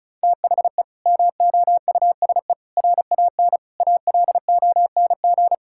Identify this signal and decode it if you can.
Morse code, heard as a keyed tone, about 34 wpm, 700 Hz. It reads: THE MOUSE RAN ALONG
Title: Sentences from Top 1000 Words - Review Part 5 - 35wpm